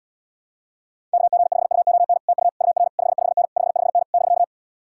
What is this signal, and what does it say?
Morse code, heard as a keyed tone, about 79 wpm, 700 Hz. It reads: KG5UON TU UR 55N 55N <BK>